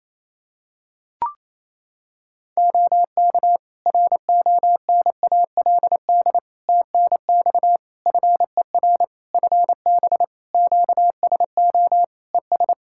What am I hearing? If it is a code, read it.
Morse code, heard as a keyed tone, about 28 wpm, 700 Hz. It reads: OK RONALD TNX FER FB QSO ES
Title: Practice QSO - Exchange 4 - Ending - 28wpm